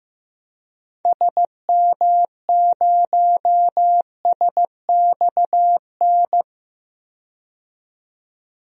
Morse code, heard as a keyed tone, about 15 wpm, 700 Hz. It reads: SM0SXN
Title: DXCC-100 Call Signs - 6 Characters - 15wpm